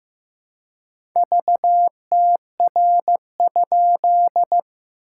Morse code, heard as a keyed tone, about 15 wpm, 700 Hz. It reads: VTR?
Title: SOTA QSO Practice 15wpm